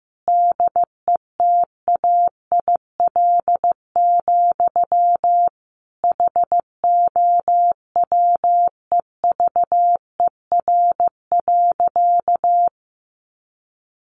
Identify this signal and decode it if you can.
Morse code, heard as a keyed tone, about 15 wpm, 700 Hz. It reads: DETAIL, HOWEVER.
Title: Sentences from Top 1500 Words 15wpm